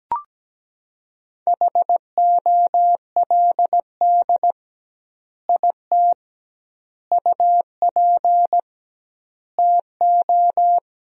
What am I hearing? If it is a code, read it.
Morse code, heard as a keyed tone, about 17 wpm, 700 Hz. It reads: HOLD IT UP TO
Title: 2x Word-Spacing - Sentences from Top 400 Words - 17wpm